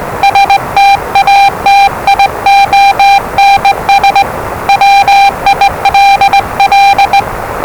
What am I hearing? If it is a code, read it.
Morse code, heard as a keyed tone, about 20 wpm, 800 Hz, with noise. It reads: STATIONS WILL
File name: WellingtonRadioZlwCloseDownOnHfAndMf / 01-Zlw 4277khz 417.5 qrt until 0925z